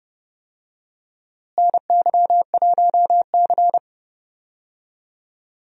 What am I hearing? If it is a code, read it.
Morse code, heard as a keyed tone, about 31 wpm, 700 Hz. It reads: NY1C
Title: USA Call Signs - 4 Characters - Repeated 2x - 30wpm